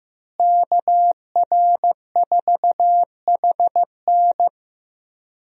Morse code, heard as a keyed tone, about 15 wpm, 700 Hz. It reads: KR4HN
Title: USA Call Signs - 5 Characters - Repeated 3x - 15wpm